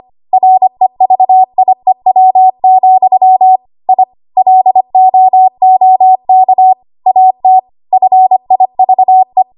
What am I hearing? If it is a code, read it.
Morse code, heard as a keyed tone, about 25 wpm, 750 Hz. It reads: REVIEW, I LOOK AT FIVE
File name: Archive / 25 WPM / 25WPM